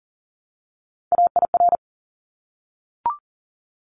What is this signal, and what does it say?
Morse code, heard as a keyed tone, about 40 wpm, 700 Hz. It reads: AIR
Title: Top 200 Words Repeated 40wpm